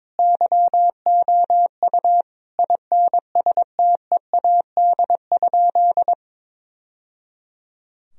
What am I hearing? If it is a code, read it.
Morse code, heard as a keyed tone, about 22 wpm, 700 Hz. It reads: YOU INSTEAD?
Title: Sentences from Top 1000 Words - Review Part 4 - 22wpm